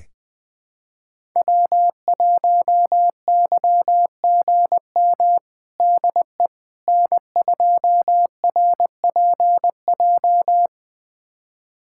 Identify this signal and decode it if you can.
Morse code, heard as a keyed tone, about 20 wpm, 700 Hz. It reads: W1YGM DE N2RPJ
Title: USA Callsign DE Callsign - 5 Characters - 20wpm